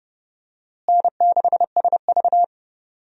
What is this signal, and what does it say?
Morse code, heard as a keyed tone, about 32 wpm, 700 Hz. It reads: N6SV
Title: USA Call Signs - 4 Characters - Repeated 3x - 30wpm